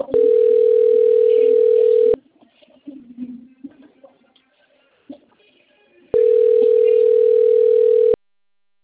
Ringback tone (North America), 440 + 480 Hz.